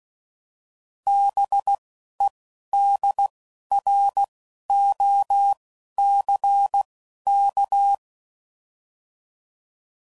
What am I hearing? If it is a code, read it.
Morse code, heard as a keyed tone, about 16 wpm, spaced as for 11 wpm, 800 Hz. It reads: BEDROCK
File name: CW Code Groups @10WPM / fromSM's / Cities